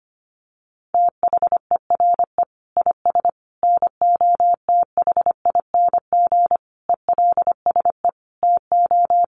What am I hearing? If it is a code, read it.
Morse code, heard as a keyed tone, about 25 wpm, 700 Hz. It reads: THERE IS NOTHING ELSE TO
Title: Sentences from Top 800 Words 25wpm